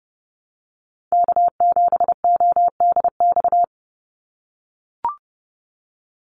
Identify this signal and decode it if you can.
Morse code, heard as a keyed tone, about 30 wpm, 700 Hz. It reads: K7ODX
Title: NA SOTA Activator Callsigns 30wpm